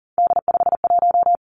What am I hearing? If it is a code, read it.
Morse code, heard as a keyed tone, about 38 wpm, 700 Hz. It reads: D51
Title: Three Challenging Letter-Number Combinations 40wpm